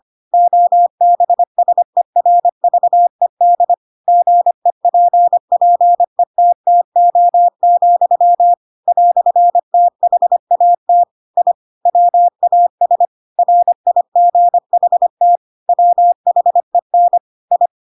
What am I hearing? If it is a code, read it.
Morse code, heard as a keyed tone, about 25 wpm, 700 Hz. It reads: OBSERVED GEPPETTO, "THAT I WAS RIGHT WHEN I
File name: Pinocchio / Pinocchio ch7 25 25 ik1hge